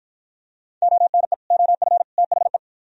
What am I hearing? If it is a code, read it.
Morse code, heard as a keyed tone, about 53 wpm, 700 Hz. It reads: ONE OF THE